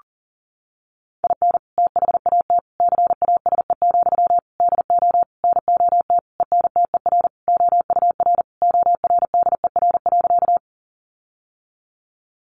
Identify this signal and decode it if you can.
Morse code, heard as a keyed tone, about 37 wpm, 700 Hz. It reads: IN THAT CASE, DO NOT ENTER OUR ORDER.
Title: Sentences from Top 1300 Words 40wpm